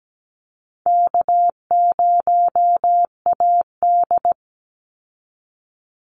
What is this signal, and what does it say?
Morse code, heard as a keyed tone, about 17 wpm, 700 Hz. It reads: K0AD
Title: Easy Call Signs - 17wpm